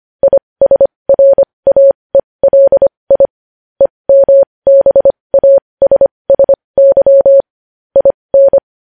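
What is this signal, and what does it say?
Morse code, heard as a keyed tone, about 25 wpm, spaced as for 20 wpm, 550 Hz. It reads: ISRAELI EMBASSY IN
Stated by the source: Morse code news headlines sent at 20.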